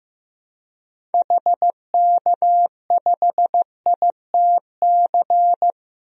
Morse code, heard as a keyed tone, about 15 wpm, 700 Hz. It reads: HK5ITC
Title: DXCC-100 Call Signs - 6 Characters - Repeated 2x - 15wpm